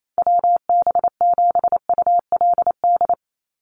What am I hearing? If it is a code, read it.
Morse code, heard as a keyed tone, about 28 wpm, 700 Hz. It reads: WB7ULD
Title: NA SOTA Chaser Callsigns 28wpm